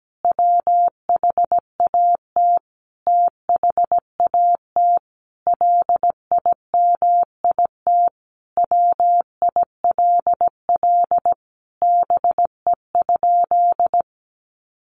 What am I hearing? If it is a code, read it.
Morse code, heard as a keyed tone, about 17 wpm, 700 Hz. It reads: WHAT THAT LIMIT WILL BE?
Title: Sentences from Top 1500 Words 17wpm